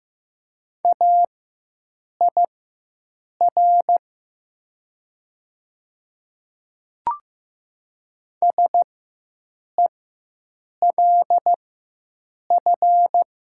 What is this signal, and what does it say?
Morse code, heard as a keyed tone, about 15 wpm, spaced as for 7 wpm, 700 Hz. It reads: AIR SELF
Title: 4x Character Spacing - Top 300 Words - 15wpm